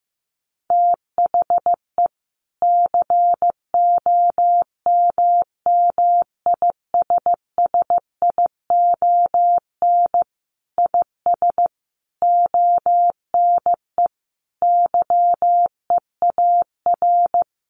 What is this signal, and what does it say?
Morse code, heard as a keyed tone, about 15 wpm, 700 Hz. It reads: THE COMMISSION IS ONE YEAR
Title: Sentences from Top 1400 Words 15wpm